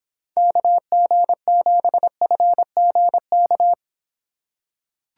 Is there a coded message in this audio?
Yes: Morse code from 0.4 to 3.7 s, heard as a keyed tone, about 26 wpm, 700 Hz. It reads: KG7FGK